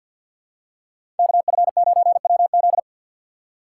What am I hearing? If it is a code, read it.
Morse code, heard as a keyed tone, about 53 wpm, 700 Hz. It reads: KF9WZ
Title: USA Call Signs - 4x Character Spacing - 50wpm